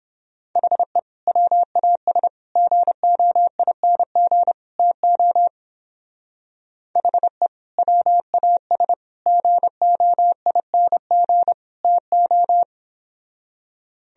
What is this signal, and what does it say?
Morse code, heard as a keyed tone, about 31 wpm, 700 Hz. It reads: HE WAS GOING TO HE WAS GOING TO